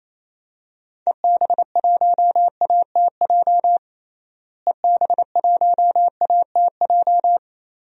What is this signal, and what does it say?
Morse code, heard as a keyed tone, about 27 wpm, 700 Hz. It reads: EB1ATJ EB1ATJ
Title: DXCC-100 Call Signs - 6 Characters - Repeated 2x - 28wpm